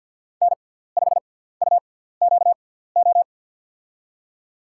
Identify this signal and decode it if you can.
Morse code, heard as a keyed tone, about 53 wpm, spaced as for 17 wpm, 700 Hz. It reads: N5UQO
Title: USA Call Signs - 5 Characters - 6x Character Spacing - 50wpm